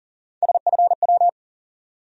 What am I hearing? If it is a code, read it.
Morse code, heard as a keyed tone, about 40 wpm, 700 Hz. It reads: SFW
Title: USA Call Sign Postfixes - 3 Letters 40wpm